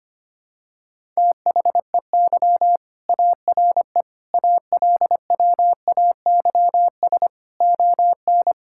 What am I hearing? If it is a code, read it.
Morse code, heard as a keyed tone, about 25 wpm, 700 Hz. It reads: THEY ARE ALWAYS ON
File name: practice / 25wpm / All-Age Appropriate Jokes - 25wpm